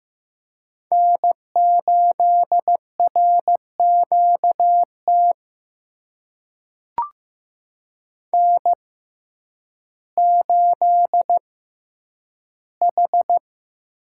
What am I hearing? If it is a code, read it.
Morse code, heard as a keyed tone, about 15 wpm, spaced as for 9 wpm, 700 Hz. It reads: N8RQT N8H
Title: USA Call Signs - 5 Characters - 6x Character Spacing - 15wpm